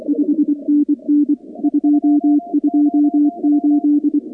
Morse code, heard as a keyed tone, about 24 wpm, 290 Hz. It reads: 5NN 228
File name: contests / 2019-cw-wpx-kt5j-k5pi- -2330 0-44